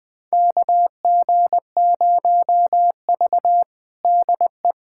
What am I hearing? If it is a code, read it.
Morse code, heard as a keyed tone, about 21 wpm, 700 Hz. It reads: KG0V DE
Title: USA Callsign DE Callsign - 4 Characters - 20wpm